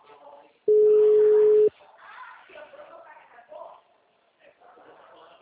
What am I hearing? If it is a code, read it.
Ringback tone (Europe, 425 Hz), with speech.